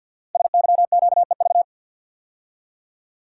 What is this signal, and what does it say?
Morse code, heard as a keyed tone, about 50 wpm, 690 Hz. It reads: SYQEV
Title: Code Groups - 5 Letters - 50wpm